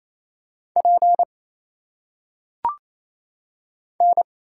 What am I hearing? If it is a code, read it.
Morse code, heard as a keyed tone, about 27 wpm, 700 Hz. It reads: P N
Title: Single Letters - Phonetic Pronunciation 28wpm